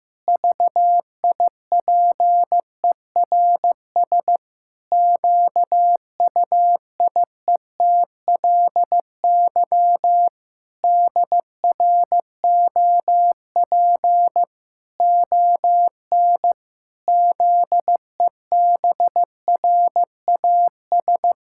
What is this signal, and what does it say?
Morse code, heard as a keyed tone, about 15 wpm, 700 Hz. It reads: VIPERS QUIETLY DROP ON ZEBRAS
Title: Pangram Challenge v2 - 15wpm